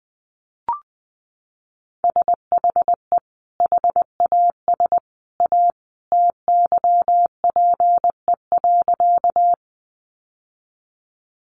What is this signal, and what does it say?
Morse code, heard as a keyed tone, about 20 wpm, 700 Hz. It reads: SHE HAS A TYPE.
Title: Sentences from Top 600 Words 20wpm